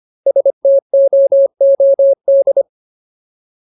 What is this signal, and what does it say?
Morse code, heard as a keyed tone, about 25 wpm, 550 Hz. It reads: STOOD
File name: Ultimate Word List / 25WPM Ultimate